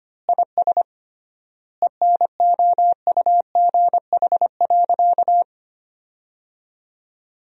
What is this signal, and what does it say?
Morse code, heard as a keyed tone, about 25 wpm, 700 Hz. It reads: IS ENOUGH.
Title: 3x Word-Spacing - Sentences from Top 400 Words - 25wpm